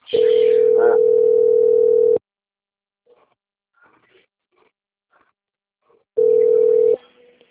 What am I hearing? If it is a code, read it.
Ringback tone (North America), 440 + 480 Hz.
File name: out- - - - - .5244